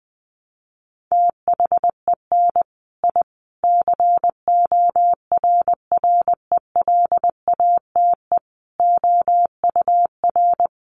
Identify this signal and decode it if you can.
Morse code, heard as a keyed tone, about 20 wpm, 700 Hz. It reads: THEN I CORRELATE OUR
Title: A Sound Of Thunder 20wpm